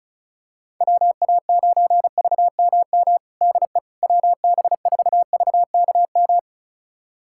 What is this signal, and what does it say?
Morse code, heard as a keyed tone, about 36 wpm, 690 Hz. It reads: WA9VMM DE WB4VKM
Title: USA Callsign DE Callsign - 6 Characters - 35wpm